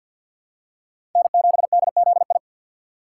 Morse code, heard as a keyed tone, about 53 wpm, 700 Hz. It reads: N7DZI
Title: USA Call Signs - 5 Characters - 3x Character Spacing - 50wpm